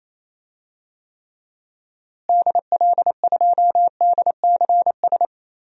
Morse code, heard as a keyed tone, about 27 wpm, 700 Hz. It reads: DL2DCS
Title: DXCC-100 Call Signs - 6 Characters - Repeated 3x - 28wpm